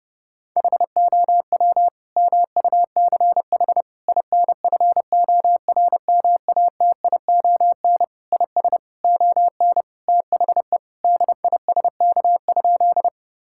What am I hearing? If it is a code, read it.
Morse code, heard as a keyed tone, about 32 wpm, 700 Hz. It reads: HOW MUCH INFORMATION IS ON THE DISK?
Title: Sentences from Top 1000 Words - Review Part 6 - 30wpm